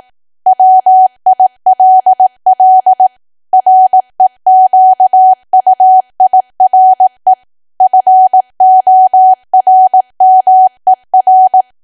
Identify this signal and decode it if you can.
Morse code, heard as a keyed tone, about 18 wpm, 750 Hz. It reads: WILL REQUIRE FORMER
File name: cwa / W1AW-Level2 / 18WPM